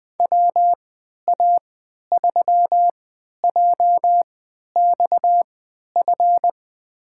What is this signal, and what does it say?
Morse code, heard as a keyed tone, about 20 wpm, spaced as for 11 wpm, 700 Hz. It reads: WA3JXF
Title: USA Call Signs - 3x Character Spacing - 20wpm